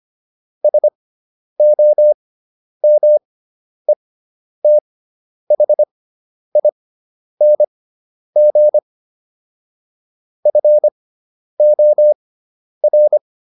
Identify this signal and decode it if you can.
Morse code, heard as a keyed tone, about 25 wpm, spaced as for 10 wpm, 600 Hz. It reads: SOMETHING FOR